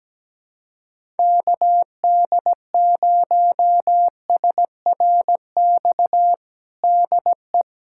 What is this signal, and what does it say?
Morse code, heard as a keyed tone, about 17 wpm, 700 Hz. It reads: KD0SRX DE
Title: USA Callsign DE Callsign - 6 Characters - 17wpm